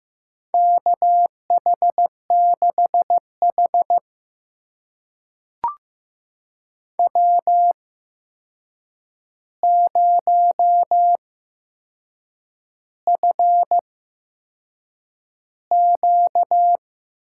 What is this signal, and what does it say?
Morse code, heard as a keyed tone, about 15 wpm, spaced as for 6 wpm, 700 Hz. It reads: KH6H W0FQ
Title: USA Call Signs - 4 Characters - 8x Character Spacing - 15wpm